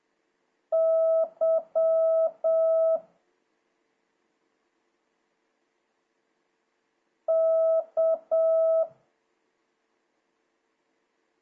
Morse code, heard as a keyed tone, about 7 wpm, 650 Hz. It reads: Y K
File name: jh8cbh / 2026cw